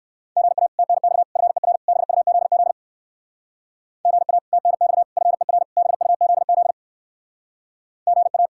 Morse code, heard as a keyed tone, about 62 wpm, 700 Hz. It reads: GA TNX FER BUZZ GA TNX FER BUZZ GA